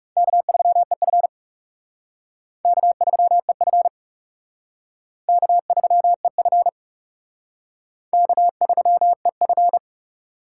Morse code, heard as a keyed tone, about 35 wpm, 700 Hz. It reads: K3EF K3EF K3EF K3EF
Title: Speed Racing - USA 1x2 Call Signs 30wpm